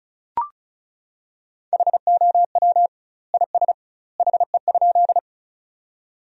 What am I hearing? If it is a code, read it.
Morse code, heard as a keyed tone, about 34 wpm, 700 Hz. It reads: HOW IS HE?
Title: 2x Word-Spacing - Sentences from Top 100 Words - 35wpm